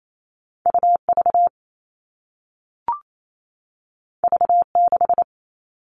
Morse code, heard as a keyed tone, about 28 wpm, 700 Hz. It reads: UV V6
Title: All Cut Numbers - 2-Digit 28wpm